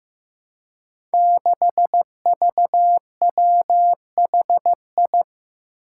Morse code, heard as a keyed tone, about 15 wpm, 700 Hz. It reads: 6VWHI
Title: Five Challenging Letter-Number Combinations 15wpm